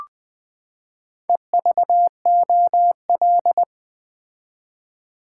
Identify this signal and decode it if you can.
Morse code, heard as a keyed tone, about 20 wpm, 700 Hz. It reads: EVOL
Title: Build A Word - 4 Letters At A Time - 12-Letter Words - 20wpm